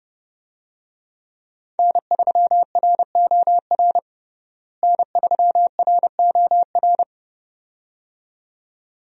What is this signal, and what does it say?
Morse code, heard as a keyed tone, about 30 wpm, 700 Hz. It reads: N3ROR N3ROR
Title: USA Call Signs - 5 Characters - Repeated 2x - 30wpm